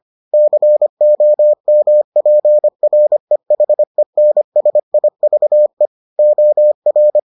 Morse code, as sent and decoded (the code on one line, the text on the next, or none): -.-. --- -- .--. .-. .... -. ... .. ...- . / --- .-.
COMPREHENSIVE OR